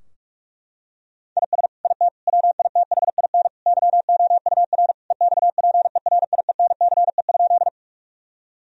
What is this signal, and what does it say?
Morse code, heard as a keyed tone, about 41 wpm, 700 Hz. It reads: IS IT WITHIN YOUR EXPERIENCE?
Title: Sentences from Top 1000 Words - Review Part 4 - 45wpm